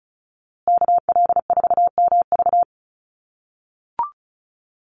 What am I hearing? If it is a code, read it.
Morse code, heard as a keyed tone, about 34 wpm, 700 Hz. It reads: KL4MV
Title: USA Call Signs Repeated 2x 35wpm